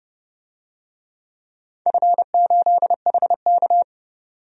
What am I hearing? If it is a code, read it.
Morse code, heard as a keyed tone, about 30 wpm, 700 Hz. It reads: F8HK
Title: DXCC-100 Call Signs - 4 Characters - 30wpm